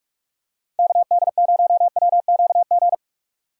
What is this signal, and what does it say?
Morse code, heard as a keyed tone, about 43 wpm, 700 Hz. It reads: KD0WQG
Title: USA Call Signs - 6 Characters - 2x Character Spacing - 45wpm